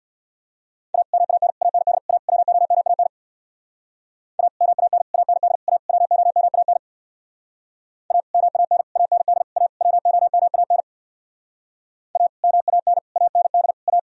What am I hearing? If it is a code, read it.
Morse code, heard as a keyed tone, about 62 wpm, 700 Hz. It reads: A MAN AND A WOMAN A MAN AND A WOMAN A MAN AND A WOMAN A MAN AND A
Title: Speed-Racing - Sets of 5 Words 50wpm